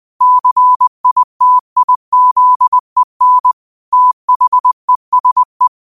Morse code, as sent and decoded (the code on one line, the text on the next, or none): -.-. .. - .. --.. -. / - .... ... .
CITIZEN THESE